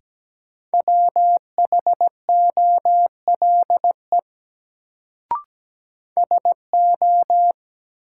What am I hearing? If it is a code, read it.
Morse code, heard as a keyed tone, about 17 wpm, 700 Hz. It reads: WHOLE SO
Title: Sets of 2 Words - New Word Drill - 17wpm